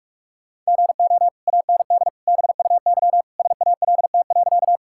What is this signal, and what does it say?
Morse code, heard as a keyed tone, about 41 wpm, 700 Hz. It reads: GO AND BUY SALT.
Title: Sentences from Top 1000 Words - Review Part 5 - 45wpm